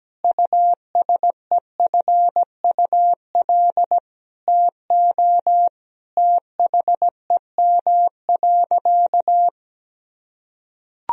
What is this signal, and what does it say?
Morse code, heard as a keyed tone, about 17 wpm, 700 Hz. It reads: USEFUL TO THEM.
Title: Sentences from Top 1000 Words - Review Part 4 - 17wpm